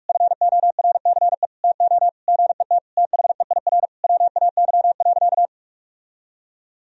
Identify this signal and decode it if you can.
Morse code, heard as a keyed tone, about 46 wpm, 700 Hz. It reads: FORCE TO GET THEIR WAY.
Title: Sentences from Top 1000 Words - Review Part 4 - 45wpm